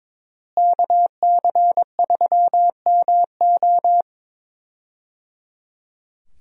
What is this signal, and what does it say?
Morse code, heard as a keyed tone, about 22 wpm, 700 Hz. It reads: KC3MO
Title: USA Call Signs - 5 Characters - Repeated 3x - 22wpm